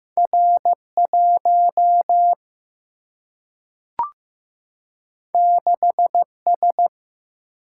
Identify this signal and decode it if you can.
Morse code, heard as a keyed tone, about 15 wpm, 700 Hz. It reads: R 1 6 S
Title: Single-Letter Single-Number 15wpm